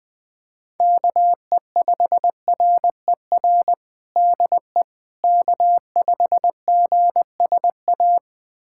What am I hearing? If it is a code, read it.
Morse code, heard as a keyed tone, about 20 wpm, 700 Hz. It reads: KE5RER DE K5GSA
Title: USA Callsign DE Callsign 20wpm